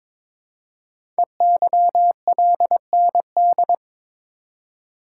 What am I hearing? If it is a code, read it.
Morse code, heard as a keyed tone, about 22 wpm, 700 Hz. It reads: EYLND
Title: Code Groups - 5 Letters - 22wpm